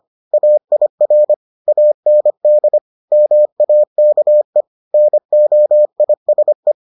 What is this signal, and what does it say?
Morse code, heard as a keyed tone, about 25 wpm, 600 Hz. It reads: AIR AND MAKE NOISE